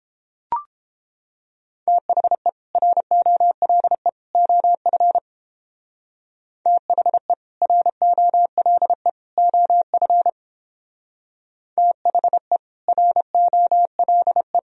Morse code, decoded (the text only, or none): THE ROLE OF THE ROLE OF THE ROLE